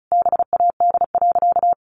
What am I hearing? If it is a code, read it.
Morse code, heard as a keyed tone, about 34 wpm, 700 Hz. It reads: BAD.
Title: Sentences from Top 1100 Words 35wpm